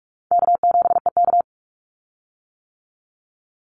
Morse code, heard as a keyed tone, about 44 wpm, 700 Hz. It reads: K7EK
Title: Easy Call Signs - 45wpm